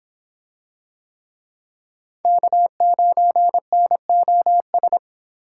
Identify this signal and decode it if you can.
Morse code, heard as a keyed tone, about 26 wpm, 700 Hz. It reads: K9NOS